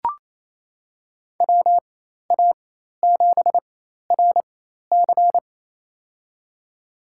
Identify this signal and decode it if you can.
Morse code, heard as a keyed tone, about 28 wpm, spaced as for 13 wpm, 700 Hz. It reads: WA7RC
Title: USA Call Signs - 5 Characters - 4x Character Spacing - 28wpm